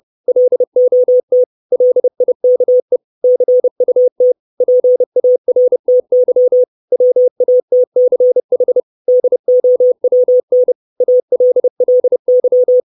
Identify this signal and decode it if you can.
Morse code, heard as a keyed tone, about 30 wpm, 500 Hz. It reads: LOT LIKE CUT PARTY WATCH DOWN ALLY
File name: 632 Words 30WPM